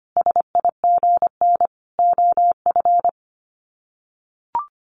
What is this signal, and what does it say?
Morse code, heard as a keyed tone, about 25 wpm, 700 Hz. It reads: SIGN OF
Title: Sets of 5 Words Encore2 25wpm